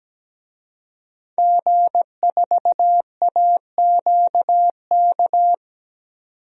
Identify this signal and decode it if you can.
Morse code, heard as a keyed tone, about 17 wpm, 700 Hz. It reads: G4AQK
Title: DXCC-100 Call Signs - 5 Characters - 17wpm